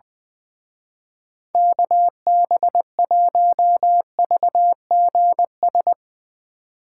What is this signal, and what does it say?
Morse code, heard as a keyed tone, about 22 wpm, 700 Hz. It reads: KB1VGS
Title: Speed Racing - USA 2x3 Call Signs 20wpm